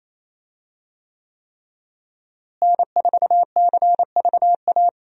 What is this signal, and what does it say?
Morse code, heard as a keyed tone, about 28 wpm, 700 Hz. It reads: N4CVA
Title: USA Call Signs - 5 Characters - Repeated 3x - 28wpm